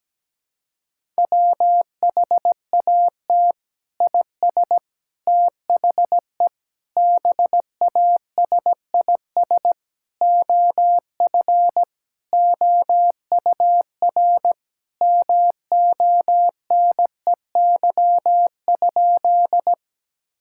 Morse code, heard as a keyed tone, about 17 wpm, 700 Hz. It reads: WHAT IS THE BASIS OF OUR MONEY?
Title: Sentences from Top 1000 Words - Review Part 5 - 17wpm